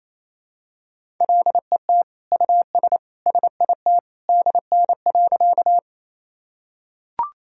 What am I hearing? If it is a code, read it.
Morse code, heard as a keyed tone, about 28 wpm, 700 Hz. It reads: LET US SIT DN.
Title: Sentences from Top 500 Words and Abbreviations 28wpm